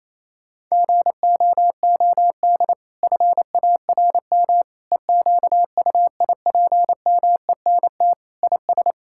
Morse code, heard as a keyed tone, about 28 wpm, 700 Hz. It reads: GOOD FARM EQUIPMENT IS
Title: Sentences from Top 1000 Words - Review Part 5 - 28wpm